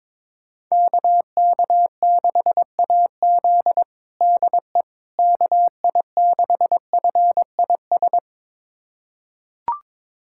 Morse code, heard as a keyed tone, about 22 wpm, 700 Hz. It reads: KK6AZ DE KI6FIS
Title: USA Callsign DE Callsign 22wpm